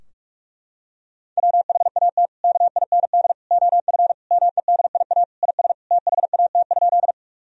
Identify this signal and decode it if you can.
Morse code, heard as a keyed tone, about 46 wpm, 700 Hz. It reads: WHAT KIND OF MEDIA IS THAT?
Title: Sentences from Top 1000 Words - Review Part 4 - 45wpm